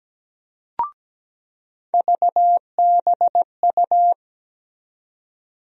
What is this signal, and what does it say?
Morse code, heard as a keyed tone, about 17 wpm, 700 Hz. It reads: VBU
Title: USA Call Sign Postfixes - 3 Letters 17wpm